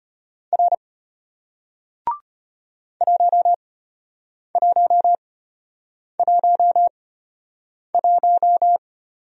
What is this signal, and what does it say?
Morse code, heard as a keyed tone, about 34 wpm, spaced as for 8 wpm, 700 Hz. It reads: R 1111